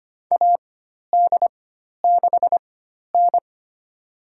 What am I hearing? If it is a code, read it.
Morse code, heard as a keyed tone, about 25 wpm, spaced as for 12 wpm, 700 Hz. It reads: AD6N
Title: USA Call Signs - 4 Characters - 4x Character Spacing - 25wpm